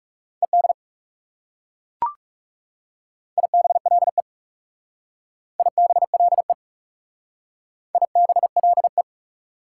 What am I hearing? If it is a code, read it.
Morse code, heard as a keyed tone, about 39 wpm, 700 Hz. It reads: ED IBLE IBLE IBLE